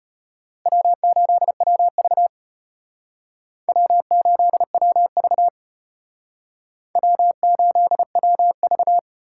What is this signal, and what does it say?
Morse code, heard as a keyed tone, about 32 wpm, 700 Hz. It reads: W8WV W8WV W8WV